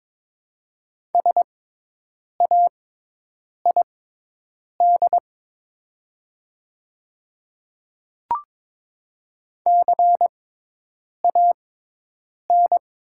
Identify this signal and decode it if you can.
Morse code, heard as a keyed tone, about 22 wpm, spaced as for 8 wpm, 700 Hz. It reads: SAID CAN
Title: 6x Character Spacing - Top 100 Words - 22wpm